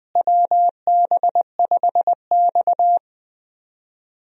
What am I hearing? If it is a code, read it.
Morse code, heard as a keyed tone, about 20 wpm, 700 Hz. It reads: WB5X
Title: USA Call Signs - 4 Characters - Repeated 3x - 20wpm